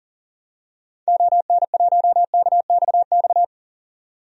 Morse code, heard as a keyed tone, about 40 wpm, 700 Hz. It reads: ON1KXX
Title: DXCC-100 Call Signs - 6 Characters - 40wpm